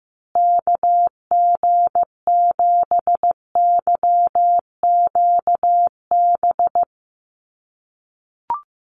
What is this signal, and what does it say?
Morse code, heard as a keyed tone, about 15 wpm, 700 Hz. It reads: KG7YQB
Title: USA Call Signs Repeated 2x 15wpm